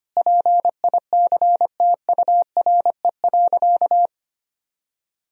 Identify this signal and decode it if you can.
Morse code, heard as a keyed tone, about 25 wpm, 700 Hz. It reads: PICTURE.
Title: Sentences from Top 1000 Words - Review Part 1 - 25wpm